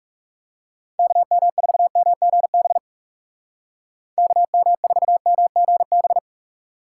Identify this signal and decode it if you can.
Morse code, heard as a keyed tone, about 38 wpm, 700 Hz. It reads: KM4MGB KM4MGB